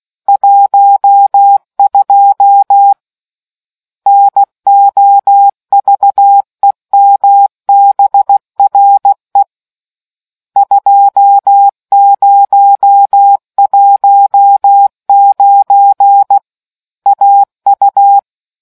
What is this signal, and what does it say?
Morse code, heard as a keyed tone, about 16 wpm, 800 Hz. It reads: 12 NOVEMBRE 2019 AU